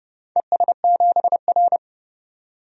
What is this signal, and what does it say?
Morse code, heard as a keyed tone, about 28 wpm, 700 Hz. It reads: ES7R
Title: DXCC-100 Call Signs - 4 Characters - 30wpm